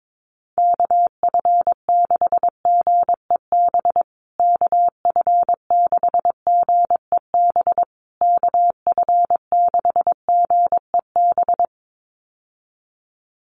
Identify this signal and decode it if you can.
Morse code, heard as a keyed tone, about 22 wpm, 700 Hz. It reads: KF6GEB KF6GEB KF6GEB
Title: USA Call Signs Repeated 3x 22wpm